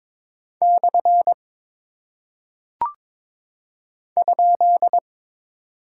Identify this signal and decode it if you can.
Morse code, heard as a keyed tone, about 22 wpm, 700 Hz. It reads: / ?
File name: practice / 22wpm / Single Punctuation-Prosign - 22wpm